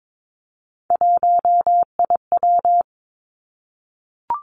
Morse code, heard as a keyed tone, about 22 wpm, 700 Hz. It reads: 1IW
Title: Three Challenging Letter-Number Combinations 22wpm